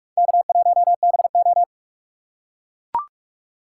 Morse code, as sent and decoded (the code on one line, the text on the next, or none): -.- .---- -... ---
K1BO